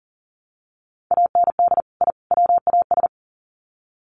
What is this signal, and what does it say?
Morse code, heard as a keyed tone, about 40 wpm, 700 Hz. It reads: AND I WAS
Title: Sets of 3 Words 40wpm